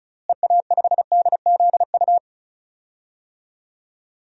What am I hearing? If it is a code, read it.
Morse code, heard as a keyed tone, about 34 wpm, 700 Hz. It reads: EA5DZU
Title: DXCC-100 Call Signs - 6 Characters - Repeated 3x - 35wpm